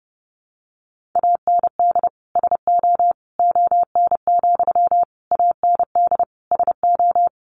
Morse code, heard as a keyed tone, about 30 wpm, 700 Hz. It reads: AND SO ON, AND SO
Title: Sentences from Top 100 Words 30wpm